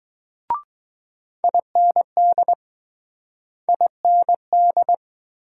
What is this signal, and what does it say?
Morse code, heard as a keyed tone, about 20 wpm, 700 Hz. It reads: IND IND